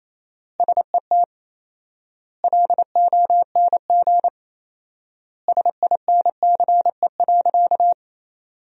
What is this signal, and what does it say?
Morse code, heard as a keyed tone, about 27 wpm, 700 Hz. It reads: SET LONG SINCE.
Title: 4x Word-Spacing - Sentences from Top 300 Words - 28wpm